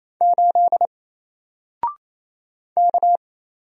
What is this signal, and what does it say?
Morse code, heard as a keyed tone, about 27 wpm, 700 Hz. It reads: 8 K
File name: Single-Letter-Number-28wpm